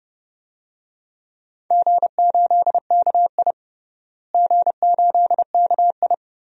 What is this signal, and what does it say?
Morse code, heard as a keyed tone, about 30 wpm, 700 Hz. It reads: G8KI G8KI
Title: DXCC-100 Call Signs - 4 Characters - Repeated 2x - 30wpm